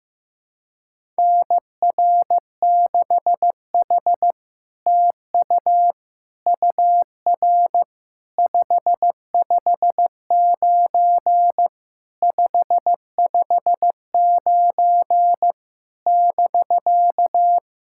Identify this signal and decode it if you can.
Morse code, heard as a keyed tone, about 15 wpm, 700 Hz. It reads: NR6H TU UR 559 559 <BK>
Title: SOTA QSO Practice 15wpm